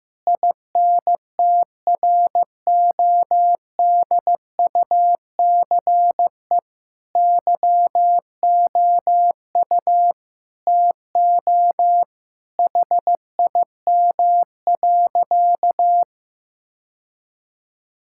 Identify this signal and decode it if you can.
Morse code, heard as a keyed tone, about 15 wpm, 700 Hz. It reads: INTRODUCE YOU TO HIM.
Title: Sentences from Top 1600 Words 15wpm